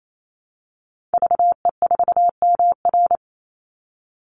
Morse code, heard as a keyed tone, about 28 wpm, 700 Hz. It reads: VE4MR
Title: Call Signs Repeated 2x - 28wpm